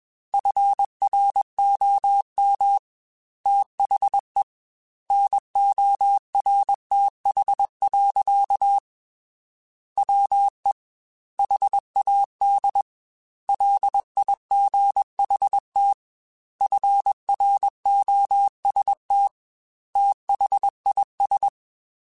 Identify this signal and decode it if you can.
Morse code, heard as a keyed tone, about 21 wpm, 800 Hz. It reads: FROM THE NORTH. WE HAD LIGHT FROST THIS
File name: Weather Sets / Weather Sets-mp3s / Weather 4 24 @20